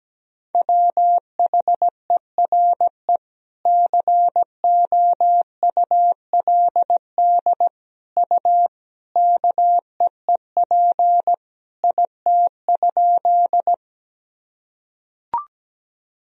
Morse code, heard as a keyed tone, about 17 wpm, 700 Hz. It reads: WHERE COULD U KEEP IT?
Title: Sentences from Top 300 Words and Abbreviations 17wpm